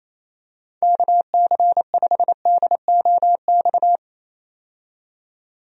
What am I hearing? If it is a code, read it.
Morse code, heard as a keyed tone, about 27 wpm, 700 Hz. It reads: KC5DOX
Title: USA Call Signs - 6 Characters - 28wpm